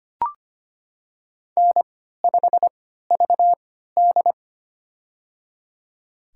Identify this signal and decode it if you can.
Morse code, heard as a keyed tone, about 25 wpm, spaced as for 14 wpm, 700 Hz. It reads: N5VD
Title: USA Call Signs - 4 Characters - 3x Character Spacing - 25wpm